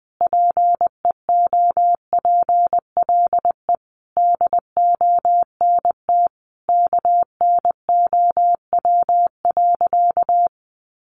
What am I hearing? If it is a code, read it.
Morse code, heard as a keyed tone, about 20 wpm, 700 Hz. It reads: PEOPLE DONT KNOW.
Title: Sentences from Top 600 Words 20wpm